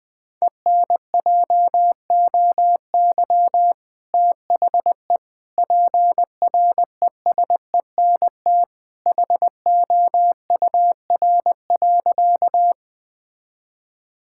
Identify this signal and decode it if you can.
Morse code, heard as a keyed tone, about 20 wpm, 700 Hz. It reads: ENJOY THE PRESENT HOUR.
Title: Sentences from Top 1000 Words - Review Part 5 - 20wpm